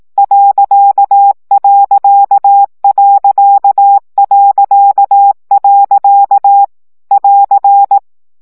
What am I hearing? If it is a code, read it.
Morse code, heard as a keyed tone, about 18 wpm, 800 Hz. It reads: ..... <AR>
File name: CW gyakorlas pont 15 baud